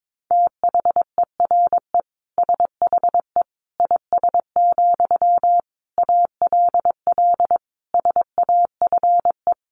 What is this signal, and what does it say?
Morse code, heard as a keyed tone, about 22 wpm, 700 Hz. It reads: THERE SHE IS, ALL SAFE
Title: Sentences from Top 900 Words 22wpm